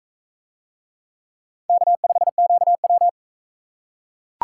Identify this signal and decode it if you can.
Morse code, heard as a keyed tone, about 37 wpm, 700 Hz. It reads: K5QW